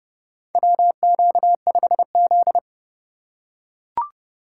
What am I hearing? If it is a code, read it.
Morse code, heard as a keyed tone, about 30 wpm, 700 Hz. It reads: WQ5Z
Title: USA Call Signs - 4 Characters - 4x Character Spacing - 30wpm